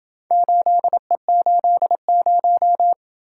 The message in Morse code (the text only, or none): 8E80